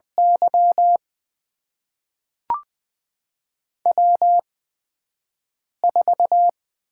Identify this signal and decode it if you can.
Morse code, heard as a keyed tone, about 20 wpm, spaced as for 5 wpm, 700 Hz. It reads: Y W4
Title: USA Call Signs - 5 Characters - 8x Character Spacing - 20wpm